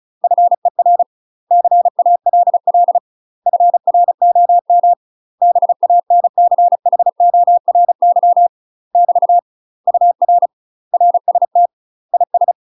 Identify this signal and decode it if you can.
Morse code, heard as a keyed tone, about 36 wpm, 700 Hz. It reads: FER CALL FROM BANCHORY <BT> UR RST IS
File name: UK MASTER QSOs TRAINING / 35WPM / UK MASTER QSOs 0036